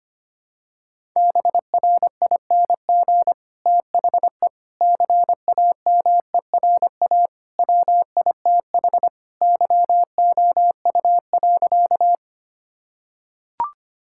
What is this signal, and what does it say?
Morse code, heard as a keyed tone, about 25 wpm, 700 Hz. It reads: BRING THE CAMERA WITH YOU.
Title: Sentences from Top 1000 Words - Review Part 5 - 25wpm